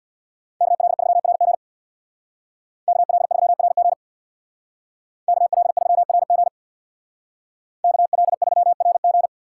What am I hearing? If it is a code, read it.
Morse code, heard as a keyed tone, about 64 wpm, 690 Hz. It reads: KL3RG KL3RG KL3RG KL3RG